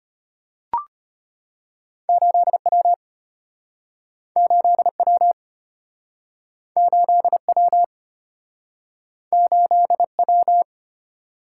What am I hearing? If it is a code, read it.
Morse code, heard as a keyed tone, about 34 wpm, 700 Hz. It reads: 8W 8W 8W 8W